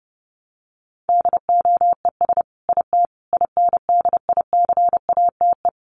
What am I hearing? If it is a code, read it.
Morse code, heard as a keyed tone, about 30 wpm, 700 Hz. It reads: DOES IT INDICATE
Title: Sentences from Top 1300 Words 30wpm